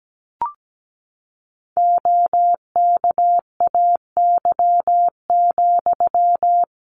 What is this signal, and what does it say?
Morse code, heard as a keyed tone, about 17 wpm, 700 Hz. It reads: OKAY,
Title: A Sound Of Thunder 17wpm